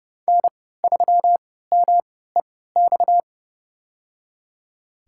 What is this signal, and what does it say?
Morse code, heard as a keyed tone, about 30 wpm, spaced as for 17 wpm, 700 Hz. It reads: N3MEX
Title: USA Call Signs - 3x Character Spacing - 30wpm